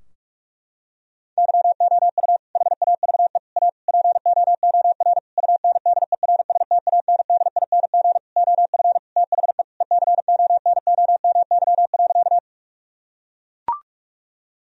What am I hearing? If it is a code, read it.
Morse code, heard as a keyed tone, about 41 wpm, 700 Hz. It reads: YOU HAVE A POOR UNDERSTANDING OF THE ECONOMY.
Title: Sentences from Top 1000 Words - Review Part 3 - 45wpm